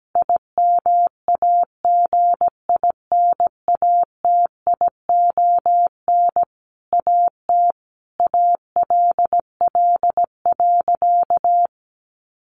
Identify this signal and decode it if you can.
Morse code, heard as a keyed tone, about 17 wpm, 700 Hz. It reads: IMAGINATION AT ALL.
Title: Sentences from Top 1300 Words 17wpm